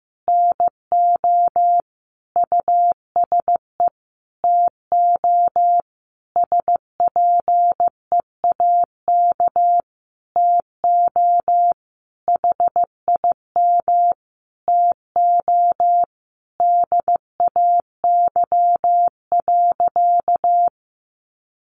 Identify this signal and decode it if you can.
Morse code, heard as a keyed tone, about 15 wpm, 700 Hz. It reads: NO USE TO SPEAK TO HIM TO DAY.
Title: Sentences from Top 1100 Words 15wpm